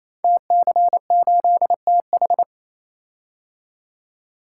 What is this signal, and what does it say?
Morse code, heard as a keyed tone, about 28 wpm, 700 Hz. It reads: TC8TH
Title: Code Groups - 5 Letter:Numbers - 28wpm